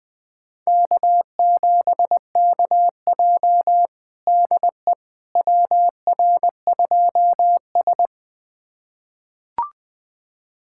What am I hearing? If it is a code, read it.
Morse code, heard as a keyed tone, about 21 wpm, 700 Hz. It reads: K7KJ DE WR2S
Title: USA Callsign DE Callsign - 4 Characters - 20wpm